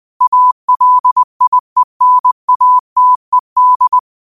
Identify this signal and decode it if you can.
Morse code, heard as a keyed tone, about 20 wpm, 1000 Hz. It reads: ALIENATED